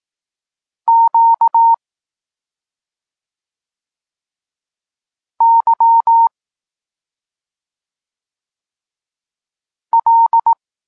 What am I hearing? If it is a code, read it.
Morse code, heard as a keyed tone, about 18 wpm, 920 Hz. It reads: Q Y L
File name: Tekst-34 4